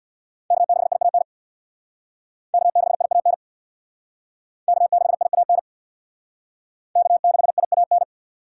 Morse code, heard as a keyed tone, about 68 wpm, 700 Hz. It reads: K6IAN K6IAN K6IAN K6IAN